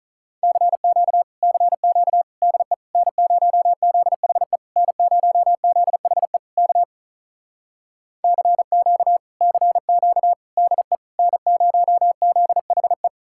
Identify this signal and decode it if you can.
Morse code, heard as a keyed tone, about 46 wpm, 700 Hz. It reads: CQ CQ DE N0ZHE N0ZHE K CQ CQ DE N0ZHE